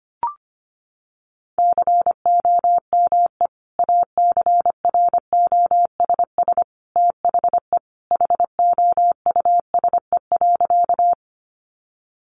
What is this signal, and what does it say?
Morse code, heard as a keyed tone, about 25 wpm, 700 Hz. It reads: COME ACROSS THE HOUSE.
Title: Sentences from Top 700 Words 25wpm